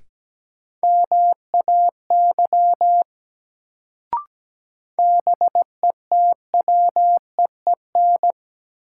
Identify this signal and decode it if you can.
Morse code, heard as a keyed tone, about 17 wpm, 700 Hz. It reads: MAY BETWEEN
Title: Sets of 2 Words - New Word Drill - 17wpm